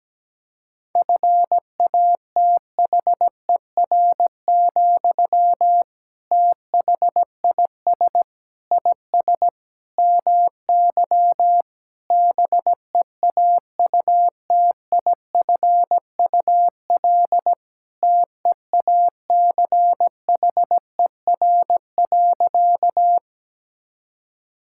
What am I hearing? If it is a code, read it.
Morse code, heard as a keyed tone, about 17 wpm, 700 Hz. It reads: FATHER, THIS IS MY BEAUTIFUL TEACHER.
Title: Sentences from Top 1000 Words - Review Part 6 - 17wpm